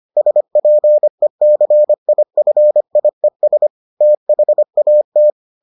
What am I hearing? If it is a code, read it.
Morse code, heard as a keyed tone, about 25 wpm, 600 Hz. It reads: SPECIFIES THAT